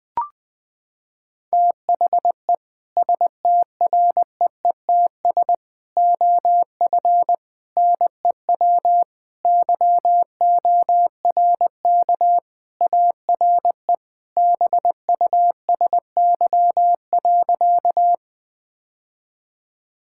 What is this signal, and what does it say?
Morse code, heard as a keyed tone, about 20 wpm, 700 Hz. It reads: THE STREETS OF NEW YORK ARE BUSY.
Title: Sentences with 3-Word Phrases - 20wpm